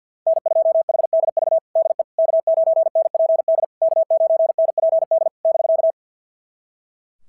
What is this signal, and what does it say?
Morse code, heard as a keyed tone, about 47 wpm, 690 Hz. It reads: N2HDV DE K9NPD K9NPD <BK>
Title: SOTA QSO Practice 50wpm